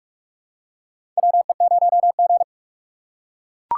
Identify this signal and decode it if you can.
Morse code, heard as a keyed tone, about 43 wpm, 700 Hz. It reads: WE0G
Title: USA Call Signs - 4 Characters - 2x Character Spacing - 45wpm